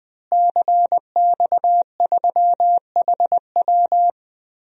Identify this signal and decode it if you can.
Morse code, heard as a keyed tone, about 20 wpm, 700 Hz. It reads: CX3HW
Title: DXCC-100 Call Signs - 5 Characters - Repeated 3x - 20wpm